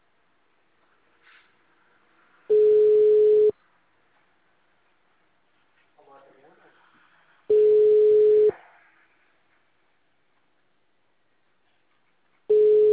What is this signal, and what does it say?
Ringback tone (Europe, 425 Hz), with speech.